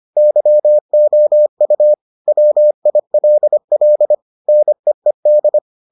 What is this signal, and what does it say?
Morse code, heard as a keyed tone, about 25 wpm, 600 Hz. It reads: YOU WILL NEED